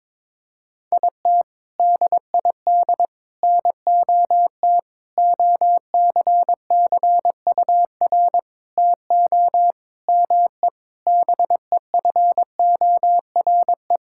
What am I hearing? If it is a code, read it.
Morse code, heard as a keyed tone, about 22 wpm, 700 Hz. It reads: IT DID NOT OCCUR TO ME BEFORE
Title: Sentences from Top 1000 Words - Review Part 6 - 22wpm